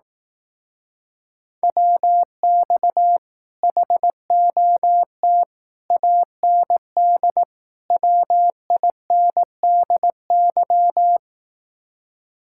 Morse code, heard as a keyed tone, about 18 wpm, 700 Hz. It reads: WX HOT AND WINDY
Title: Speed-Racing Common QSO Phrases Encore - 15wpm